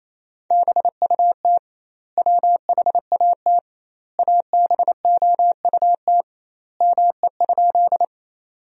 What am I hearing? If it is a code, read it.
Morse code, heard as a keyed tone, about 28 wpm, 700 Hz. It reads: BUT WHAT ABOUT ME?
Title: 2x Word-Spacing - Sentences from Top 200 Words - 28wpm